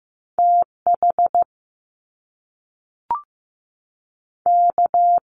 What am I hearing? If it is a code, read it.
Morse code, heard as a keyed tone, about 15 wpm, 700 Hz. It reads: T H K
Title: Top Two Letter Combinations in Words 15wpm